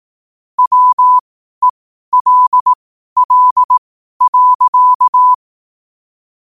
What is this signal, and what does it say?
Morse code, heard as a keyed tone, about 18 wpm, spaced as for 12 wpm, 1000 Hz. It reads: WELL.
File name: qotdfiles / 12wpm